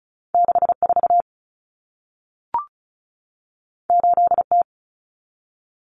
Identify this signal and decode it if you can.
Morse code, heard as a keyed tone, about 36 wpm, 700 Hz. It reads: 64 8T
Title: Common Cut Numbers - 2-Digit 35wpm